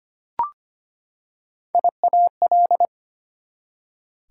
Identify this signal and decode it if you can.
Morse code, heard as a keyed tone, about 25 wpm, 700 Hz. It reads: IAL
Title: Prefixes and Suffixes 25wpm